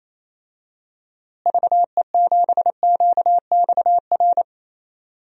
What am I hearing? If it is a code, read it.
Morse code, heard as a keyed tone, about 28 wpm, 700 Hz. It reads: VE7QXR
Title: DXCC-100 Call Signs - 6 Characters - Repeated 2x - 28wpm